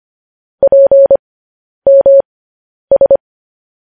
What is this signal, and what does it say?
Morse code, heard as a keyed tone, about 25 wpm, spaced as for 10 wpm, 550 Hz. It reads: PMS
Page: news-headlines-in-morse-code-at-10-wpm